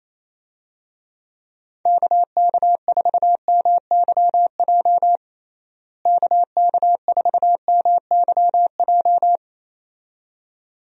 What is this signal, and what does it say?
Morse code, heard as a keyed tone, about 27 wpm, 700 Hz. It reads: KK4MYJ KK4MYJ
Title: USA Call Signs - 6 Characters - Repeated 2x - 28wpm